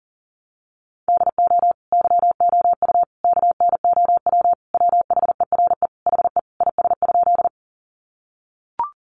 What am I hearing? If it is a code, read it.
Morse code, heard as a keyed tone, about 40 wpm, 700 Hz. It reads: DO YOU KNOW WHERE HE IS?
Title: Sentences from Top 300 Words 40wpm